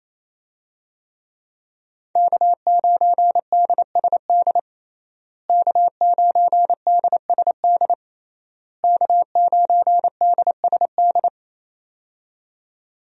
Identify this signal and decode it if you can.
Morse code, heard as a keyed tone, about 28 wpm, 700 Hz. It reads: K9DSD K9DSD K9DSD
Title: USA Call Signs - 5 Characters - Repeated 3x - 28wpm